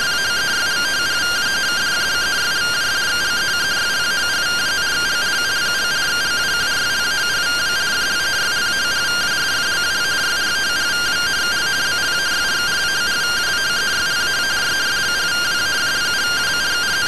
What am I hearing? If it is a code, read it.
Two-tone teleprinter signal, about 170 Hz shift.
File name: findings / CT AXIAL -377.gen-12 rtty rx